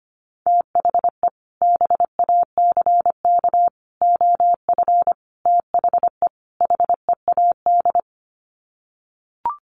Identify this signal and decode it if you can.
Morse code, heard as a keyed tone, about 25 wpm, 700 Hz. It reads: THE BACK OF THE HEAD
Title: Sets of 5 Words Encore2 25wpm